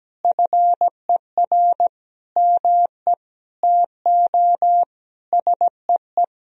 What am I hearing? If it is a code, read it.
Morse code, heard as a keyed tone, about 17 wpm, 700 Hz. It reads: FER ME TO SEE
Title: Sentences from Top 500 Words and Abbreviations 17wpm